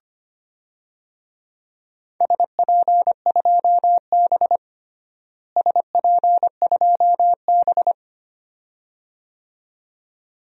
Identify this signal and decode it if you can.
Morse code, heard as a keyed tone, about 25 wpm, 700 Hz. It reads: SP2B SP2B
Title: DXCC-100 Call Signs - 4 Characters - Repeated 2x - 25wpm